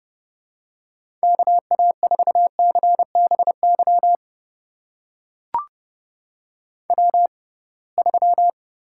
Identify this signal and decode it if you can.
Morse code, heard as a keyed tone, about 31 wpm, spaced as for 23 wpm, 700 Hz. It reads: KA4CBY W3
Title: USA Call Signs - 6x Character Spacing - 30wpm